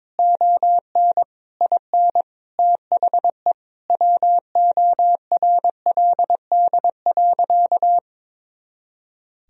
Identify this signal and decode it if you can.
Morse code, heard as a keyed tone, about 22 wpm, 700 Hz. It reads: ON IN THE WORLD.
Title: Sentences from Top 200 Words and Abbreviations 22wpm